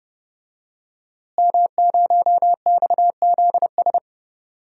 Morse code, heard as a keyed tone, about 30 wpm, 700 Hz. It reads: M0XZS
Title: DXCC-100 Call Signs - 5 Characters - Repeated 2x - 30wpm